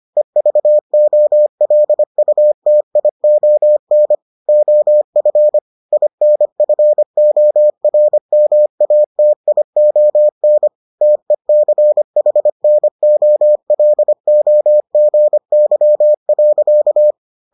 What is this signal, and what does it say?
Morse code, heard as a keyed tone, about 25 wpm, 600 Hz. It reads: EVOLUTION OF INFORMATION TECHNOLOGY.